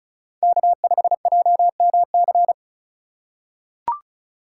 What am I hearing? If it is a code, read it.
Morse code, heard as a keyed tone, about 35 wpm, 700 Hz. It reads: K5JMC
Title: USA Call Signs - 5 Characters - 2x Character Spacing - 35wpm